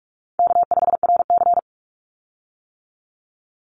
Morse code, heard as a keyed tone, about 46 wpm, 700 Hz. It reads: K5RC
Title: Easy Call Signs - 45wpm